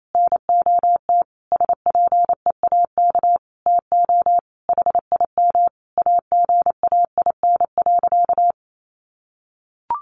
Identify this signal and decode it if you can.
Morse code, heard as a keyed tone, about 28 wpm, 700 Hz. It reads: NOT SPEAK TO HIM AGAIN.
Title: Sentences from Top 1500 Words 28wpm